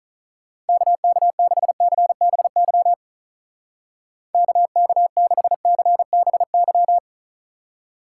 Morse code, heard as a keyed tone, about 42 wpm, 700 Hz. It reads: KK6CBY KK6CBY
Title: Speed Racing - USA 2x3 Call Signs 35wpm